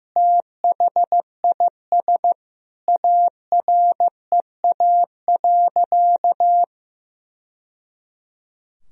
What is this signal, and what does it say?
Morse code, heard as a keyed tone, about 16 wpm, 700 Hz. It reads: THIS AREA.
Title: Sentences from Top 1000 Words - Review Part 2 - 15wpm